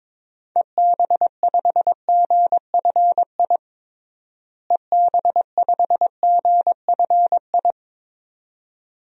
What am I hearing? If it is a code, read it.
Morse code, heard as a keyed tone, about 22 wpm, 700 Hz. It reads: EB5GFI EB5GFI
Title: DXCC-100 Call Signs - 6 Characters - Repeated 2x - 22wpm